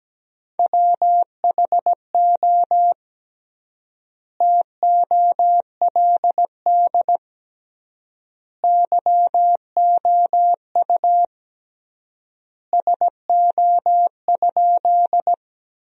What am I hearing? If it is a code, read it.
Morse code, heard as a keyed tone, about 17 wpm, 700 Hz. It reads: WHO TOLD YOU SO?
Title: 3x Word-Spacing - Sentences from Top 400 Words - 17wpm